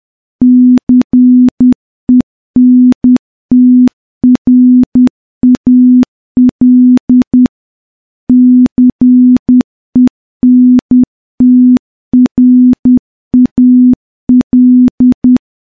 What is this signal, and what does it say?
Morse code, heard as a keyed tone, about 10 wpm, 250 Hz. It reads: CENTRAL CENTRAL
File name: morse-training-10WPM